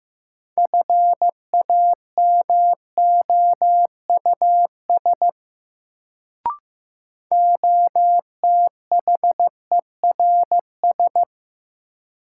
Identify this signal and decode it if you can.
Morse code, heard as a keyed tone, about 15 wpm, 700 Hz. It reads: FAMOUS OTHERS
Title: Sentences from Top 1500 Words - New Word Drill Alt - 15wpm